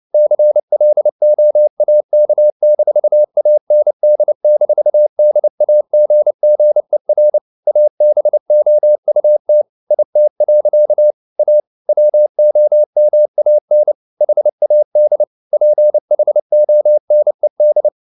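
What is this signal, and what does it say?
Morse code, heard as a keyed tone, about 29 wpm, 600 Hz. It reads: CLOAK-AND-DAGGER ABOUT IT. A WOMAN HAD PHONED
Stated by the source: Morse code at 29 wpm, sending Ken Follett's novel Never in English, from CW med Gnister.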